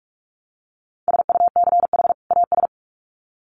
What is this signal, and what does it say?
Morse code, heard as a keyed tone, about 44 wpm, 700 Hz. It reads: SUCH AS
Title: Sets of 2 Words 45wpm